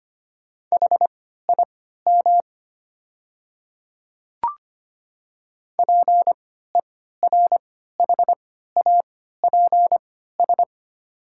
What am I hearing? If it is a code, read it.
Morse code, heard as a keyed tone, about 25 wpm, spaced as for 14 wpm, 700 Hz. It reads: HIM PERHAPS
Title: 3x Character Spacing - Top 500 Words - 25wpm